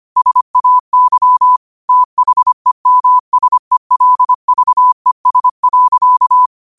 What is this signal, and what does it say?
Morse code, heard as a keyed tone, about 25 wpm, 1000 Hz. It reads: SAY THEMSELVES.
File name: qotdfiles / 25wpm